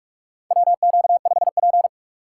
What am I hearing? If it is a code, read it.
Morse code, heard as a keyed tone, about 46 wpm, 700 Hz. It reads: WQ5P
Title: USA Call Signs - 4 Characters - 4x Character Spacing - 45wpm